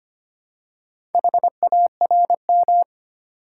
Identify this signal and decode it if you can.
Morse code, heard as a keyed tone, about 25 wpm, 700 Hz. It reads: HARM
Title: Four Letter Words - 25wpm